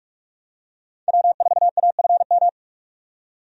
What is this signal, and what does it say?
Morse code, heard as a keyed tone, about 46 wpm, 700 Hz. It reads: W4AFM
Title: USA Call Signs - 5 Characters - 45wpm